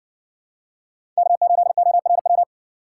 Morse code, heard as a keyed tone, about 60 wpm, 700 Hz. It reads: K8ORW